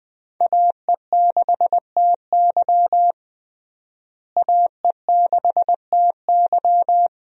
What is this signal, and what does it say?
Morse code, heard as a keyed tone, about 20 wpm, 700 Hz. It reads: AE6TY AE6TY
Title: USA Call Signs - 5 Characters - Repeated 2x - 20wpm